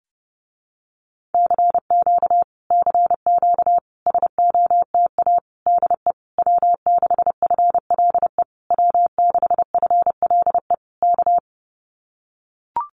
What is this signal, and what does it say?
Morse code, heard as a keyed tone, about 28 wpm, 700 Hz. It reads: CQ CQ SOTA DE W6FLE W6FLE K
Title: Common QSO Phrases Encore 30wpm